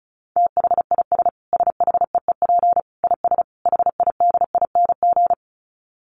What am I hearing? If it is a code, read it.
Morse code, heard as a keyed tone, about 34 wpm, 700 Hz. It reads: THIS SHEEP IS HIDING
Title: Dit Challenge 35wpm